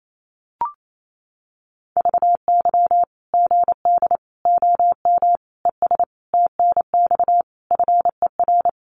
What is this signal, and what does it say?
Morse code, heard as a keyed tone, about 28 wpm, 700 Hz. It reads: VY GD OM ES TNX FER
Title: Common QSO Phrases 28wpm